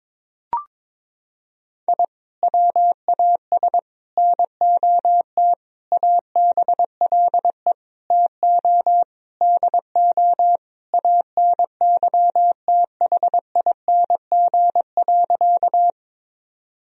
Morse code, heard as a keyed tone, about 21 wpm, 700 Hz. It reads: I WAS NOT ABLE TO DO ANYTHING.
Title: Sentences from Top 1000 Words - Review Part 2 - 22wpm